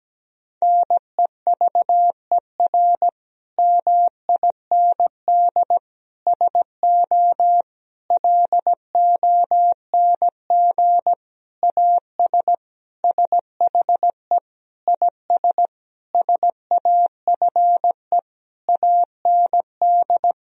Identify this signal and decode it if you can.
Morse code, heard as a keyed tone, about 17 wpm, 700 Hz. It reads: NEVER MIND SO LONG AS SHE IS SAFE AND
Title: Sentences from Top 1000 Words - Review Part 1 - 17wpm